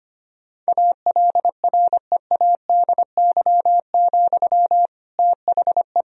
Morse code, heard as a keyed tone, about 25 wpm, 700 Hz. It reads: ALREADY, THE
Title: Sentences from Top 1000 Words - Review Part 4 - 25wpm